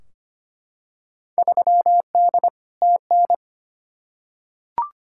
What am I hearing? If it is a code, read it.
Morse code, heard as a keyed tone, about 25 wpm, 700 Hz. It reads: 3D TN
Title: Field Day Exchanges Encore - 25wpm